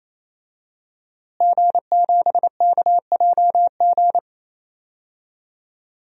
Morse code, heard as a keyed tone, about 28 wpm, 700 Hz. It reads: G7KJG
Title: DXCC-100 Call Signs - 5 Characters - Repeated 3x - 28wpm